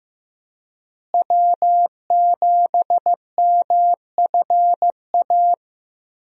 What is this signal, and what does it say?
Morse code, heard as a keyed tone, about 15 wpm, 700 Hz. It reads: W7MFA
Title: USA Call Signs - 5 Characters - Repeated 2x - 15wpm